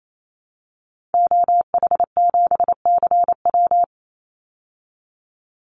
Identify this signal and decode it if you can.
Morse code, heard as a keyed tone, about 28 wpm, 700 Hz. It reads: OH7CW
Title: Call Signs Repeated 3x - 28wpm